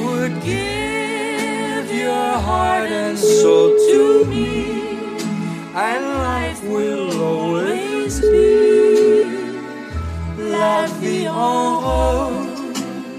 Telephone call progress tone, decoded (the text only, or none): ringback tone